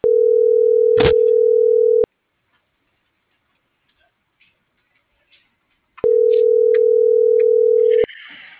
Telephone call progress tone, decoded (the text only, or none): ringback tone